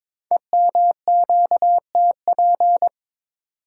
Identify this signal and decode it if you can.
Morse code, heard as a keyed tone, about 22 wpm, 700 Hz. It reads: EMQTP
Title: Code Groups - 5 Letters - 22wpm